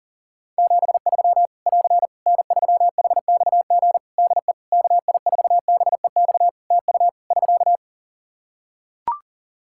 Morse code, heard as a keyed tone, about 40 wpm, 700 Hz. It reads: 73 <AR> N3HXG DE KI4BEX TU <SK>
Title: K1USN SST QSO Practice 40wpm